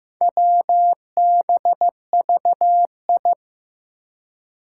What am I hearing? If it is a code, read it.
Morse code, heard as a keyed tone, about 15 wpm, 700 Hz. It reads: WBVI
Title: Four Challenging Letter-Number Combinations 15wpm